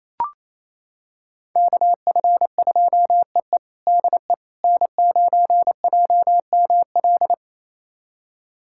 Morse code, heard as a keyed tone, about 28 wpm, 700 Hz. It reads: KF2EE DE N9JML
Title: USA Callsign DE Callsign - 5 Characters - 28wpm